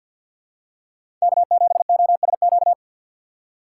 Morse code, heard as a keyed tone, about 53 wpm, 700 Hz. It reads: K7OSQ
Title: USA Call Signs - 5 Characters - 4x Character Spacing - 50wpm